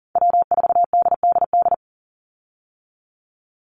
Morse code, heard as a keyed tone, about 41 wpm, 700 Hz. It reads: W4DDD
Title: Call Signs - 40wpm